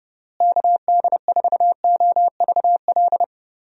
Morse code, heard as a keyed tone, about 29 wpm, 700 Hz. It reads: KD4OVL
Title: USA Call Signs - 6 Characters - Repeated 3x - 30wpm